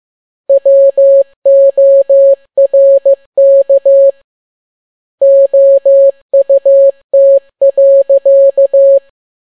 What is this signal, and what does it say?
Morse code, heard as a keyed tone, about 15 wpm, 550 Hz. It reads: WORK OUT.
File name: n6tt p1 15C13